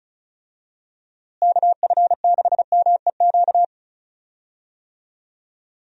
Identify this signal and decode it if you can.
Morse code, heard as a keyed tone, about 34 wpm, 700 Hz. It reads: KF6MEQ
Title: USA Call Signs - 6 Characters - Repeated 2x - 35wpm